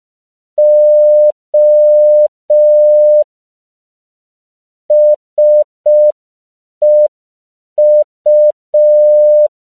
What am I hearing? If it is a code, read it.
Morse code, heard as a keyed tone, about 5 wpm, 600 Hz. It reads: O SEU
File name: CW / 5PPM